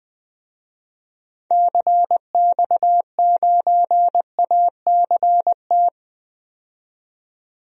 Morse code, heard as a keyed tone, about 20 wpm, 700 Hz. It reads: CX9ACT
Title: DXCC-100 Call Signs - 6 Characters - 20wpm